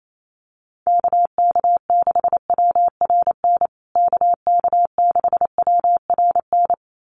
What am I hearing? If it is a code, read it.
Morse code, heard as a keyed tone, about 28 wpm, 700 Hz. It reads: KK6WRN KK6WRN
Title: USA Call Signs Repeated 2x 28wpm